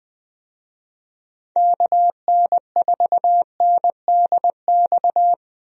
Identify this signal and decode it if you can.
Morse code, heard as a keyed tone, about 20 wpm, 700 Hz. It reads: KN4NDX
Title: USA Call Signs - 6 Characters - 20wpm